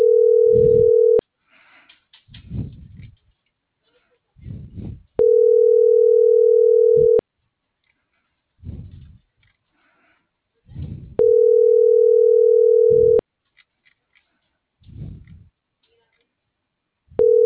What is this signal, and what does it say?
Ringback tone (North America), 440 + 480 Hz.